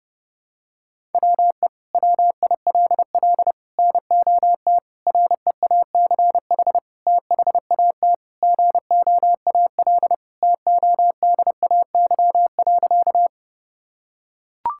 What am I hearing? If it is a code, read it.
Morse code, heard as a keyed tone, about 30 wpm, 700 Hz. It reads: WE WILL NOT REACH THAT GOAL TODAY.
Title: Sentences from Top 1000 Words - Review Part 5 - 30wpm